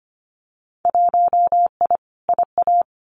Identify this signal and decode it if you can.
Morse code, heard as a keyed tone, about 25 wpm, 700 Hz. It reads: 1I IA
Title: Winter Field Day Exchanges 25wpm